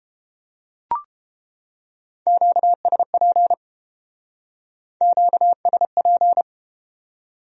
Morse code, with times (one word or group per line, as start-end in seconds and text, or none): 2.3-3.5 QSP
5.0-6.4 QSP